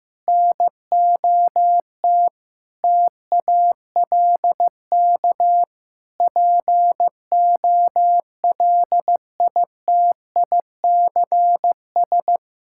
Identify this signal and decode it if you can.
Morse code, heard as a keyed tone, about 15 wpm, 700 Hz. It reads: NOT TALK POLITICS
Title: Sentences from Top 1000 Words - Review Part 6 - 15wpm